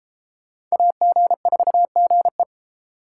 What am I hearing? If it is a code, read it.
Morse code, heard as a keyed tone, about 33 wpm, 700 Hz. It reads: AG4GE